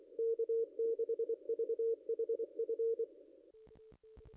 Morse code, heard as a keyed tone, about 24 wpm, 460 Hz. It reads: K6VHF
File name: contests / ss-cw-k5tr-op-n5rz -0500